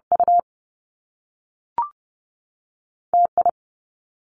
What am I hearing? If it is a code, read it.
Morse code, heard as a keyed tone, about 30 wpm, spaced as for 6 wpm, 700 Hz. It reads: UTI
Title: Two Challenging Letter-Number Combinations 30wpm